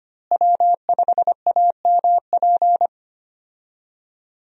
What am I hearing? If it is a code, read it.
Morse code, heard as a keyed tone, about 25 wpm, 700 Hz. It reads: W5AMP
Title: USA Call Signs - 5 Characters - Repeated 2x - 25wpm